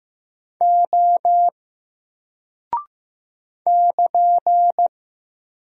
Morse code, heard as a keyed tone, about 15 wpm, 700 Hz. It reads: O <KN>
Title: Single Character (Letter:Number:Punctuation:Prosign) Alt 15wpm